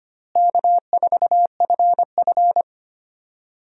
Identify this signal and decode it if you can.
Morse code, heard as a keyed tone, about 25 wpm, 700 Hz. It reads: K4FF
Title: USA Call Signs - 4 Characters - Repeated 3x - 25wpm